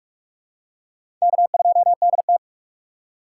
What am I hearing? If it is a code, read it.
Morse code, heard as a keyed tone, about 44 wpm, 690 Hz. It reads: K2DT
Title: USA Call Signs - 4 Characters - 3x Character Spacing - 45wpm